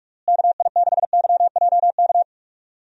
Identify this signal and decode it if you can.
Morse code, heard as a keyed tone, about 41 wpm, 700 Hz. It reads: KI6YJK
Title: USA Call Signs - 6 Characters - Repeated 3x - 45wpm